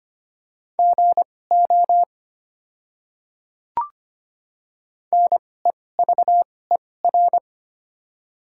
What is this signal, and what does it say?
Morse code, heard as a keyed tone, about 25 wpm, spaced as for 18 wpm, 700 Hz. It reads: GO NEVER
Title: 2x Character Spacing - Top 300 Words - 25wpm